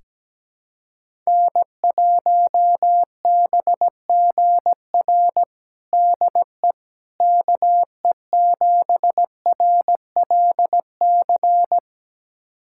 Morse code, heard as a keyed tone, about 17 wpm, 700 Hz. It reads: N1BGR DE KE7RLC
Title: USA Callsign DE Callsign 17wpm